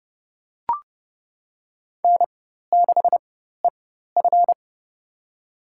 Morse code, heard as a keyed tone, about 30 wpm, spaced as for 14 wpm, 700 Hz. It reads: N6EF
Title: USA Call Signs - 4x Character Spacing - 30wpm